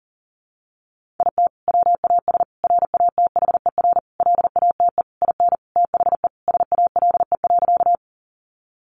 Morse code, heard as a keyed tone, about 40 wpm, 700 Hz. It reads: IT WAS RATHER LATE IN THE SALE.
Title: Sentences from Top 1400 Words 40wpm